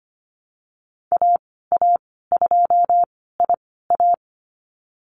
Morse code, heard as a keyed tone, about 25 wpm, spaced as for 16 wpm, 700 Hz. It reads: AA2IA
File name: Lesson 15g - Number 5 - Call Signs 25wpm